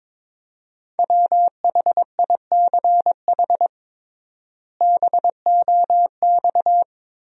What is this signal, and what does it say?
Morse code, heard as a keyed tone, about 22 wpm, 700 Hz. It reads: WHICH BOX
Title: 3x Word-Spacing - Sentences from Top 500 Words - 22wpm